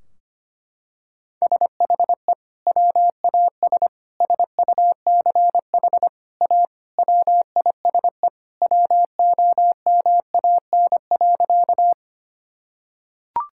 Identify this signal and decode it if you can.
Morse code, heard as a keyed tone, about 25 wpm, 700 Hz. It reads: SHE WAS SUCH A WISE WOMAN.
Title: Sentences from Top 1600 Words 25wpm